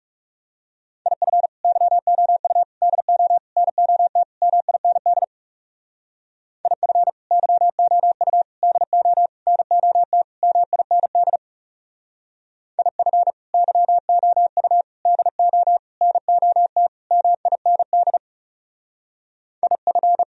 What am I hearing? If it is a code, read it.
Morse code, heard as a keyed tone, about 37 wpm, 700 Hz. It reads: IF YOU DO NOT MIND IF YOU DO NOT MIND IF YOU DO NOT MIND IF
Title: Speed-Racing - Sets of 5 Words 30wpm